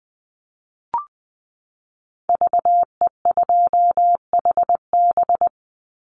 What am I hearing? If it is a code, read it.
Morse code, heard as a keyed tone, about 20 wpm, 700 Hz. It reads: VE2HB
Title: Call Signs - 20wpm